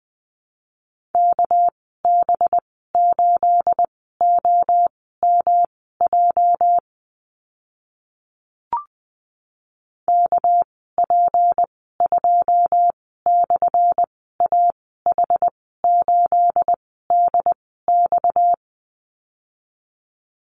Morse code, heard as a keyed tone, about 20 wpm, spaced as for 15 wpm, 700 Hz. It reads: KB8OMJ KP2/AH8DX
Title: 245 Lesson 39g - Letter z - Call Signs 20wpm